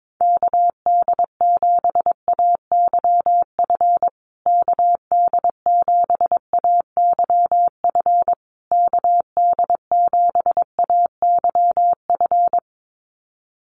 Morse code, heard as a keyed tone, about 22 wpm, 700 Hz. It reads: KD7AYF KD7AYF KD7AYF
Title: USA Call Signs Repeated 3x 22wpm